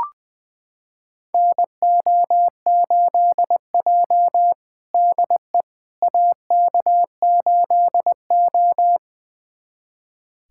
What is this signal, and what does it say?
Morse code, heard as a keyed tone, about 20 wpm, 700 Hz. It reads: NO8J DE AK8O
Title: USA Callsign DE Callsign - 4 Characters - 20wpm